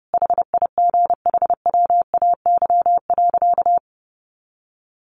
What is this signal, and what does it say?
Morse code, heard as a keyed tone, about 30 wpm, 700 Hz. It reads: HIGHWAY.
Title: Sentences from Top 1400 Words 30wpm